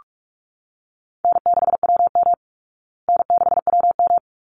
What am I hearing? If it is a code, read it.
Morse code, heard as a keyed tone, about 44 wpm, 700 Hz. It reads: N6WM N6WM
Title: Call Signs Repeated 3x - 45wpm